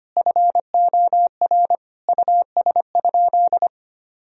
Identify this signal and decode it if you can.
Morse code, heard as a keyed tone, about 25 wpm, 700 Hz. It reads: FOR US?
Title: Sentences from Top 1000 Words - Review Part 3 - 25wpm